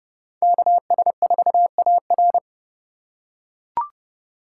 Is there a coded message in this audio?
Yes: Morse code from 0.4 to 2.4 s, heard as a keyed tone, about 29 wpm, 700 Hz. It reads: KS4AR